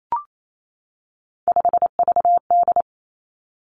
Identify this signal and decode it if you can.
Morse code, heard as a keyed tone, about 28 wpm, 700 Hz. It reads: 5VD
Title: Three Challenging Letter-Number Combinations 28wpm